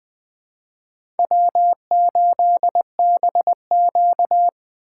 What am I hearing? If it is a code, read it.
Morse code, heard as a keyed tone, about 20 wpm, 700 Hz. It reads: W8BQ
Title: USA Call Signs - 4x Character Spacing - 20wpm